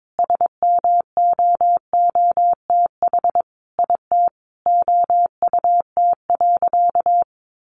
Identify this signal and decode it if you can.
Morse code, heard as a keyed tone, about 22 wpm, 700 Hz. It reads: SMOOTH IT OUT.
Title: Sentences from Top 1400 Words 22wpm